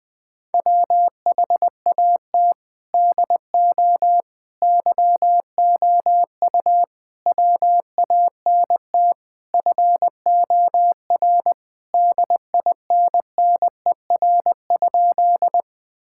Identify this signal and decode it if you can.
Morse code, heard as a keyed tone, about 20 wpm, 700 Hz. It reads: WHAT DO YOU WANT FOR DINNER?
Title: Sentences with 4-Word Phrases - 20wpm